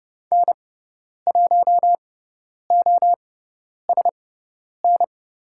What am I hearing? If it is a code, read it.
Morse code, heard as a keyed tone, about 30 wpm, spaced as for 10 wpm, 700 Hz. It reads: N1OSN
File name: Lesson 18g - Letter m - Call Signs 30wpm